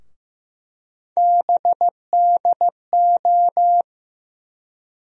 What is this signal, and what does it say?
Morse code, heard as a keyed tone, about 15 wpm, 700 Hz. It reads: BDO
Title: USA Call Sign Postfixes - 3 Letters 15wpm